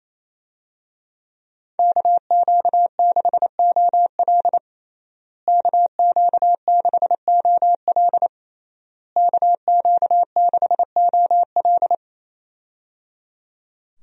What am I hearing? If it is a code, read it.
Morse code, heard as a keyed tone, about 28 wpm, 700 Hz. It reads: KQ6OL KQ6OL KQ6OL
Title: USA Call Signs - 5 Characters - Repeated 3x - 28wpm